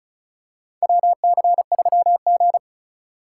Morse code, heard as a keyed tone, about 36 wpm, 700 Hz. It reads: WC3G
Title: USA Call Signs - 4 Characters - 3x Character Spacing - 35wpm